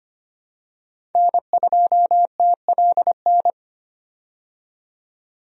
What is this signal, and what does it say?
Morse code, heard as a keyed tone, about 25 wpm, 700 Hz. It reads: N2TLN
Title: USA Call Signs - 5 Characters - 25wpm